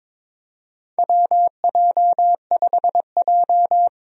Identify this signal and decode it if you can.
Morse code, heard as a keyed tone, about 22 wpm, 700 Hz. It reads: WJ5J
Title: USA Call Signs - 4 Characters - Repeated 2x - 22wpm